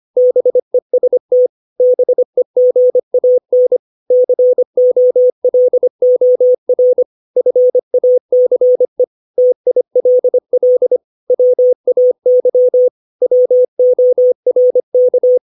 Morse code, heard as a keyed tone, about 25 wpm, 500 Hz. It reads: BEST BEGAN COLOR FACE TILL WAY WORK
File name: 632 words / 632 Words 25WPM